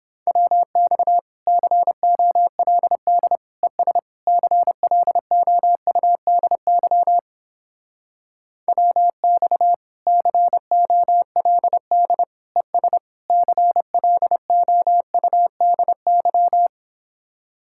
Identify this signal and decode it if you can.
Morse code, heard as a keyed tone, about 29 wpm, 700 Hz. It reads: WX COLD ES CLOUDY WX COLD ES CLOUDY